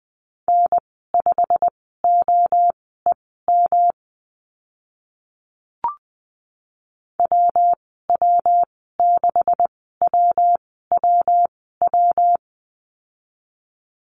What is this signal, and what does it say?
Morse code, heard as a keyed tone, about 20 wpm, spaced as for 14 wpm, 700 Hz. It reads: N5OEM WW6WWW
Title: 140 Lesson 23g - Letter f - Call Signs 20wpm